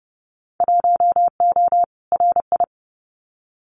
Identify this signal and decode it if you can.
Morse code, heard as a keyed tone, about 30 wpm, 700 Hz. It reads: 1O RI
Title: Winter Field Day Exchanges 30wpm